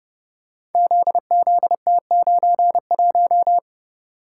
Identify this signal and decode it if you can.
Morse code, heard as a keyed tone, about 30 wpm, 700 Hz. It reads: ZZT91
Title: Code Groups - 5 Letter:Numbers - 30wpm